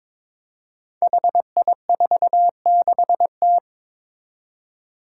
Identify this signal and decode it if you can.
Morse code, heard as a keyed tone, about 22 wpm, 700 Hz. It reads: HI46T
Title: Five Challenging Letter-Number Combinations 22wpm